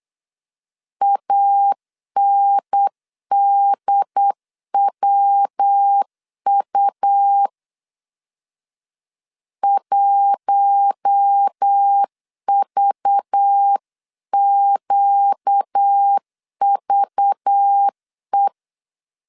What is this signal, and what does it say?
Morse code, heard as a keyed tone, about 8 wpm, 800 Hz. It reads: ANDWU 1VQVE
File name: 8wpm